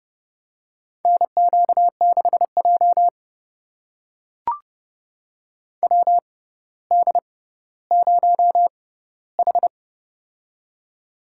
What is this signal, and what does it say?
Morse code, heard as a keyed tone, about 30 wpm, spaced as for 15 wpm, 700 Hz. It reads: NQ6J WD0H
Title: USA Call Signs - 4 Characters - 6x Character Spacing - 30wpm